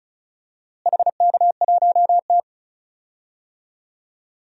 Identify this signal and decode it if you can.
Morse code, heard as a keyed tone, about 34 wpm, 700 Hz. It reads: HK1T
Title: DXCC-100 Call Signs - 4 Characters - Repeated 3x - 35wpm